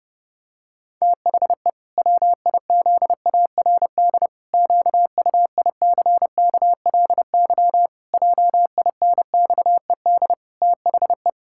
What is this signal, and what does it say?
Morse code, heard as a keyed tone, about 32 wpm, 700 Hz. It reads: THE WIZARD QUICKLY JINXED THE
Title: Pangram Challenge v2 - 30wpm